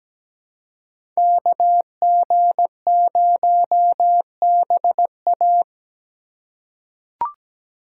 Morse code, heard as a keyed tone, about 17 wpm, 700 Hz. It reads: KG0BA
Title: USA Call Signs - 5 Characters - 3x Character Spacing - 17wpm